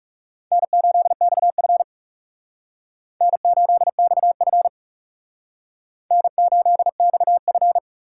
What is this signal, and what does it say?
Morse code, heard as a keyed tone, about 36 wpm, 700 Hz. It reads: N8XF N8XF N8XF
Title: Speed Racing - USA Call Signs 30wpm